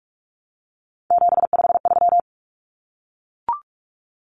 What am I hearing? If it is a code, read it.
Morse code, heard as a keyed tone, about 44 wpm, 700 Hz. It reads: 753
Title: Common Cut Numbers - 3-Digit 45wpm